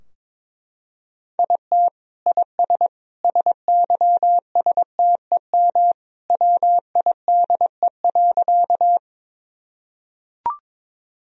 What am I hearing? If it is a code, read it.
Morse code, heard as a keyed tone, about 22 wpm, 700 Hz. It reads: IT IS SYSTEM WIDE.
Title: Sentences from Top 1000 Words - Review Part 3 - 22wpm